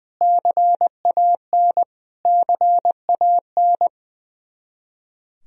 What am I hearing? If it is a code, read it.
Morse code, heard as a keyed tone, about 20 wpm, 700 Hz. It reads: CAN CAN
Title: Top 100 Words Repeated 20wpm